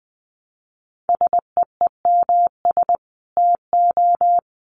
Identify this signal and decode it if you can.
Morse code, heard as a keyed tone, about 20 wpm, 700 Hz. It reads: SEEMS TO
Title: Dit Challenge 20wpm